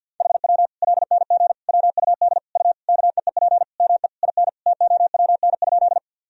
Morse code, heard as a keyed tone, about 47 wpm, 700 Hz. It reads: HW LNG WUD U KEEP ME IN TOWN?
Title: Sentences from Top 500 Words and Abbreviations 50wpm